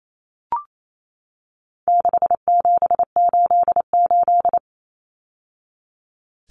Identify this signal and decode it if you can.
Morse code, heard as a keyed tone, about 28 wpm, 700 Hz. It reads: 6788
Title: Numbers - 4-Digit 28wpm